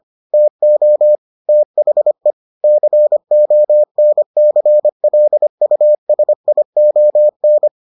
Morse code, heard as a keyed tone, about 25 wpm, 600 Hz. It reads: TO THE CONCLUSION